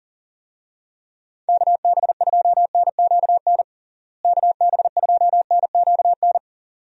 Morse code, heard as a keyed tone, about 40 wpm, 700 Hz. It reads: KB2NQN KB2NQN
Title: USA Call Signs - 6 Characters - Repeated 2x - 40wpm